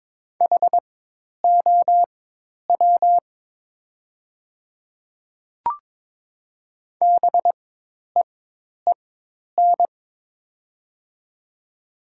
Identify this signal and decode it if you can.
Morse code, heard as a keyed tone, about 22 wpm, spaced as for 10 wpm, 700 Hz. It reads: HOW BEEN
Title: 4x Character Spacing - Top 100 Words - 22wpm